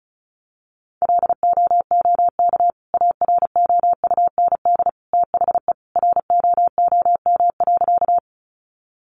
Morse code, heard as a keyed tone, about 34 wpm, 700 Hz. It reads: LOOK AROUND THE ROOM.
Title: Sentences from Top 600 Words 35wpm